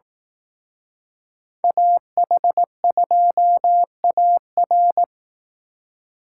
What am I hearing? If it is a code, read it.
Morse code, heard as a keyed tone, about 18 wpm, 700 Hz. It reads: AH2AR
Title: Speed Racing - USA 2x2 Call Signs 15wpm